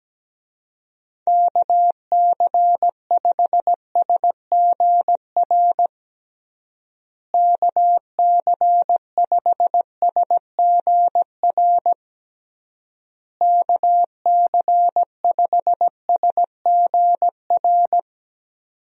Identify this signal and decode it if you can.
Morse code, heard as a keyed tone, about 17 wpm, 700 Hz. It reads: KC5SGR KC5SGR KC5SGR
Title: USA Call Signs - 6 Characters - Repeated 3x - 17wpm